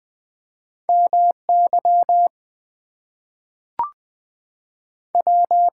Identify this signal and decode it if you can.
Morse code, heard as a keyed tone, about 20 wpm, 700 Hz. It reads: M Y W
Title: Two Letter Words - 20wpm